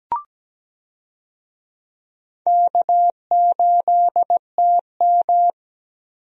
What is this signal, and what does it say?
Morse code, heard as a keyed tone, about 17 wpm, 700 Hz. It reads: K8TM
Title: USA Call Signs - 4 Characters - Repeated 2x - 17wpm